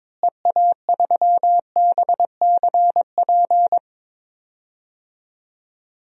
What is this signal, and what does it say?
Morse code, heard as a keyed tone, about 21 wpm, 700 Hz. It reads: EA3BCP
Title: DXCC-100 Call Signs - 6 Characters - Repeated 3x - 22wpm